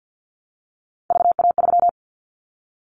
Morse code, heard as a keyed tone, about 50 wpm, 700 Hz. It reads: VA3
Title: Callsign Prefixes for Top 25 DXCC - Repeated 3x - 50wpm